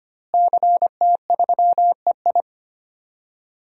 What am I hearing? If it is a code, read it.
Morse code, heard as a keyed tone, about 25 wpm, 700 Hz. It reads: CT3EI
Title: DXCC-100 Call Signs - 5 Characters - Repeated 2x - 25wpm